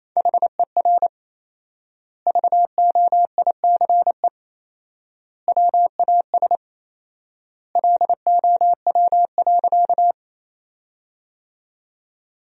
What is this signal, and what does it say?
Morse code, heard as a keyed tone, about 28 wpm, 700 Hz. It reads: HER VOICE WAS LOW.
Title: 4x Word-Spacing - Sentences from Top 500 Words - 28wpm